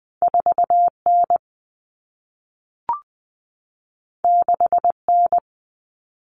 Morse code, heard as a keyed tone, about 20 wpm, 700 Hz. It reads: 4N 6N
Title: Common Cut Numbers - 2-Digit 20wpm